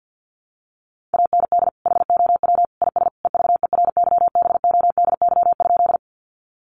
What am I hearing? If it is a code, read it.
Morse code, heard as a keyed tone, about 48 wpm, 700 Hz. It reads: AND HOW IS EVERYBODY?
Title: Sentences from Top 1400 Words 50wpm